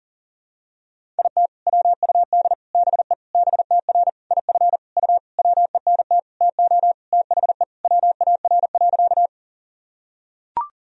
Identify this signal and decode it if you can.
Morse code, heard as a keyed tone, about 38 wpm, 700 Hz. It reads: IT WUD BE BTR IF U WENT TO THE WAR.
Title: Sentences from Top 400 Words and Abbreviations 40wpm